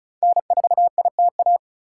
Morse code, heard as a keyed tone, about 34 wpm, 700 Hz. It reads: N4ITA
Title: USA Call Signs - 5 Characters - 3x Character Spacing - 35wpm